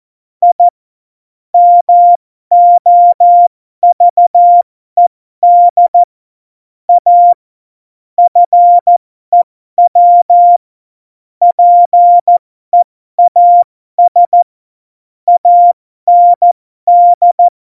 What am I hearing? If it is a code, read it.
Morse code, heard as a keyed tone, about 14 wpm, spaced as for 12 wpm, 700 Hz. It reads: I MOVED A FEW PEAS AND
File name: Morsekurs / Witze